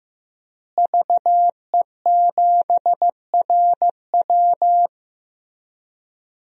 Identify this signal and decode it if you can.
Morse code, heard as a keyed tone, about 15 wpm, 700 Hz. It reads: VE7RW
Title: DXCC-100 Call Signs - 5 Characters - Repeated 3x - 15wpm